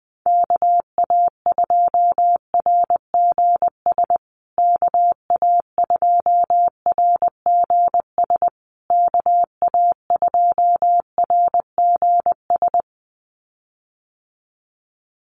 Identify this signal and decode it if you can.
Morse code, heard as a keyed tone, about 20 wpm, 700 Hz. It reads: KA2RGS KA2RGS KA2RGS
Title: USA Call Signs Repeated 3x 20wpm